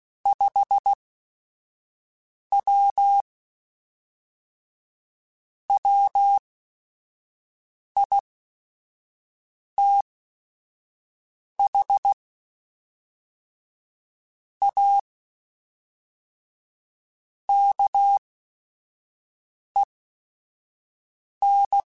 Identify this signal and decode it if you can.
Morse code, heard as a keyed tone, about 16 wpm, spaced as for 5 wpm, 800 Hz. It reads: 5W WITH A KEN
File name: QSOs / QSOs-mp3s / WD0BGP 1@5